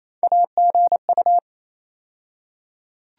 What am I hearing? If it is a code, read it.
Morse code, heard as a keyed tone, about 28 wpm, 700 Hz. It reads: AGU
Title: USA Call Sign Postfixes - 3 Letters 28wpm